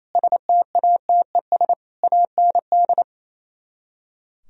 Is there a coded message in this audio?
Yes: Morse code from 0.1 to 3.0 s, heard as a keyed tone, about 36 wpm, 700 Hz. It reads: STATES AND